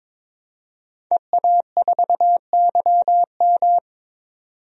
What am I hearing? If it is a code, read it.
Morse code, heard as a keyed tone, about 22 wpm, 700 Hz. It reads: EA4YM
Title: DXCC-100 Call Signs - 5 Characters - 22wpm